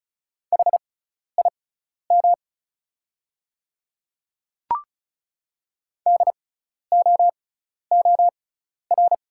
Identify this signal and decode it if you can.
Morse code, heard as a keyed tone, about 36 wpm, spaced as for 12 wpm, 700 Hz. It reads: HIM DOOR
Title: 6x Character Spacing - Top 300 Words - 35wpm